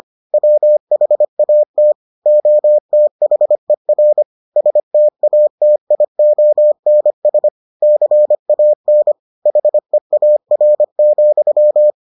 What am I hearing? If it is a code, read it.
Morse code, heard as a keyed tone, about 25 wpm, 600 Hz. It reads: WHAT OTHER STATIONS CAN HEAR,